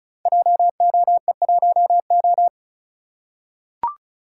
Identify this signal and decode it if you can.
Morse code, heard as a keyed tone, about 36 wpm, 700 Hz. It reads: JOE1O
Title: Code Groups - 5 Letter:Numbers - 35wpm